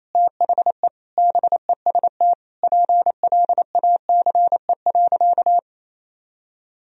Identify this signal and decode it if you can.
Morse code, heard as a keyed tone, about 28 wpm, 700 Hz. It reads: THE BEST PLACE.
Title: Sentences from Top 1000 Words - Review Part 4 - 28wpm